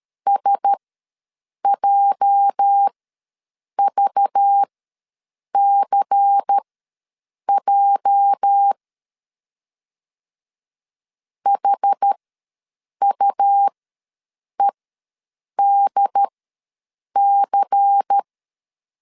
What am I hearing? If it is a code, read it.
Morse code, heard as a keyed tone, about 13 wpm, spaced as for 7 wpm, 800 Hz. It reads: SJVCJ HUEDC
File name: LEZIONI / 8 YJ45 / 8 YJ ws 12wpm 3342